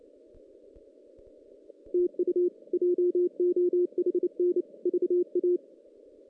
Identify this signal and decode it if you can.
Morse code, heard as a keyed tone, about 29 wpm, 360 Hz. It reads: TU JOHN VA
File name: 2016 JAN CW NAQP / 2016 Jan naqp k5ot - 0-5